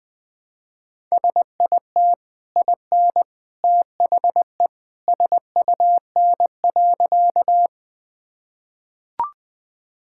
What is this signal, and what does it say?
Morse code, heard as a keyed tone, about 20 wpm, 700 Hz. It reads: SIT IN THE SUN.
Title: Easy Sentences with Hints - 20wpm